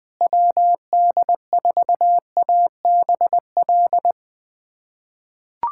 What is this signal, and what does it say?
Morse code, heard as a keyed tone, about 21 wpm, 700 Hz. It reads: WD4ABL
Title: USA Callsign DE Callsign 20wpm